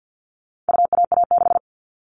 Morse code, heard as a keyed tone, about 50 wpm, 700 Hz. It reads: UAA6
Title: All Cut Numbers - 4-Digit 50wpm